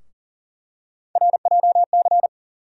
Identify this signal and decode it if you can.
Morse code, heard as a keyed tone, about 40 wpm, 700 Hz. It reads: RJC
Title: USA Call Sign Postfixes - 3 Letters 40wpm